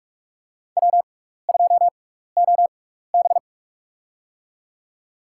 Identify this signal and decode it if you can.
Morse code, heard as a keyed tone, about 45 wpm, spaced as for 15 wpm, 700 Hz. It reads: W2OB
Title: USA Call Signs - 4 Characters - 6x Character Spacing - 45wpm